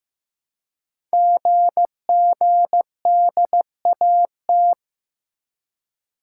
Morse code, heard as a keyed tone, about 15 wpm, 690 Hz. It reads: GGDAT
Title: Code Groups - 5 Letters - 15wpm